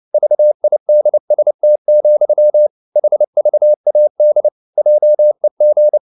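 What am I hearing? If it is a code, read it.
Morse code, heard as a keyed tone, about 29 wpm, 600 Hz. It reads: VIDST, HVAD JEG
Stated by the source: Fast (QRQ) Morse at 29 wpm, from CW med Gnister.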